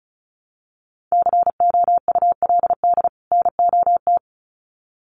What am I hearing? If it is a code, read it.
Morse code, heard as a keyed tone, about 34 wpm, 700 Hz. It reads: COULD NOT
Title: Sets of 2 Words 35wpm